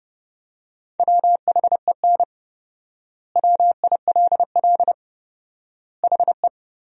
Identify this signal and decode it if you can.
Morse code, heard as a keyed tone, about 32 wpm, 700 Hz. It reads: WHEN WILL HE
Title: 4x Word-Spacing - Sentences from Top 200 Words - 30wpm